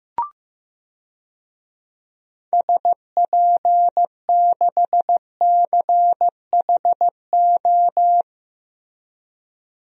Morse code, heard as a keyed tone, about 15 wpm, 700 Hz. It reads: SP6CHO
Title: DXCC-100 Call Signs - 6 Characters - Repeated 3x - 15wpm